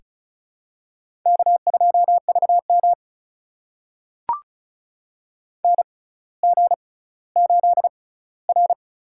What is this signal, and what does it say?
Morse code, heard as a keyed tone, about 36 wpm, spaced as for 18 wpm, 700 Hz. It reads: K2VM NG8R
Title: USA Call Signs - 4 Characters - 6x Character Spacing - 35wpm